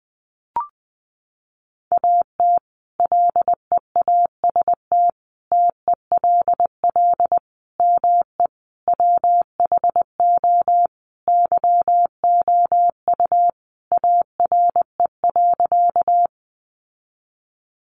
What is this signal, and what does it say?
Morse code, heard as a keyed tone, about 19 wpm, 700 Hz. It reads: AT LEAST TELL ME WHO YOU ARE.
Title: Sentences from Top 700 Words 20wpm